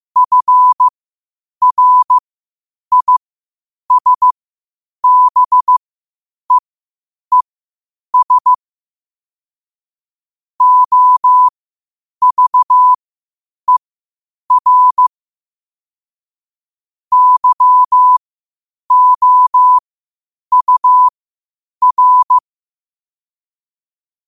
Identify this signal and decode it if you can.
Morse code, heard as a keyed tone, about 15 wpm, spaced as for 8 wpm, 1000 Hz. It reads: FRISBEES OVER YOUR
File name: qotdfiles / 8wpm